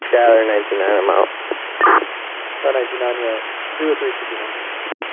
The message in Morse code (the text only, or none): none